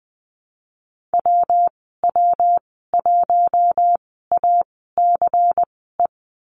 Morse code, heard as a keyed tone, about 20 wpm, spaced as for 15 wpm, 700 Hz. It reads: WW1ACE
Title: 122 Lesson 20g - Number 3 - Call Signs 20wpm